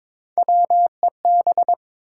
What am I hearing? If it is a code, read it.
Morse code, heard as a keyed tone, about 22 wpm, 700 Hz. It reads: WEB
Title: Sentences from Top 1000 Words - Review Part 4 - New Word Drill - 22wpm